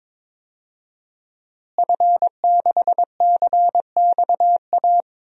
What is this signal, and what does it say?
Morse code, heard as a keyed tone, about 22 wpm, 700 Hz. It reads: F6CXA
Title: DXCC-100 Call Signs - 5 Characters - Repeated 3x - 22wpm